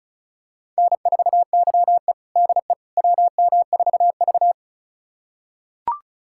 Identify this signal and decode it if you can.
Morse code, heard as a keyed tone, about 36 wpm, 690 Hz. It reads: N4YE DE WM4V
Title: USA Callsign DE Callsign - 4 Characters - 35wpm